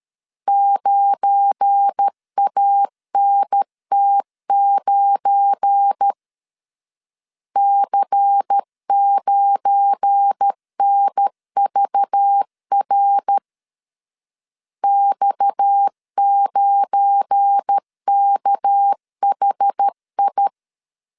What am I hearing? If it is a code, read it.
Morse code, heard as a keyed tone, about 13 wpm, 800 Hz. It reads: 9ANT9 C9NVR X9KHI
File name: LEZIONI / 7 GF9Z / 7 GF9 12wpm 2319